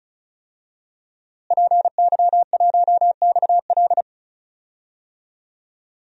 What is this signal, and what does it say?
Morse code, heard as a keyed tone, about 34 wpm, 700 Hz. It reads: PY1XL
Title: DXCC-100 Call Signs - 5 Characters - Repeated 2x - 35wpm